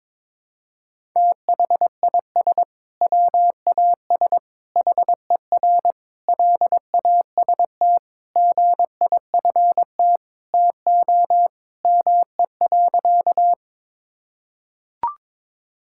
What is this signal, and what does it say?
Morse code, heard as a keyed tone, about 22 wpm, 700 Hz. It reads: THIS WAS HER LAST GIFT TO ME.
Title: Sentences from Top 1000 Words - Review Part 6 - 22wpm